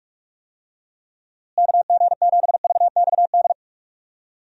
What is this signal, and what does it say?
Morse code, heard as a keyed tone, about 46 wpm, 700 Hz. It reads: KG7VXD
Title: USA Call Signs - 6 Characters - Repeated 3x - 45wpm